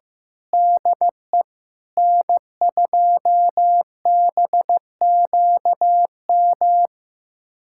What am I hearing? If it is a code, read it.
Morse code, heard as a keyed tone, about 15 wpm, 700 Hz. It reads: DE N2BQM
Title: USA Callsign DE Callsign - 5 Characters - 15wpm